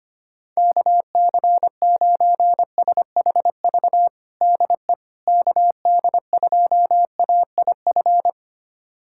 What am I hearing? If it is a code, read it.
Morse code, heard as a keyed tone, about 25 wpm, 700 Hz. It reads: KC9SHV DE KD2AIF
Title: USA Callsign DE Callsign - 6 Characters - 25wpm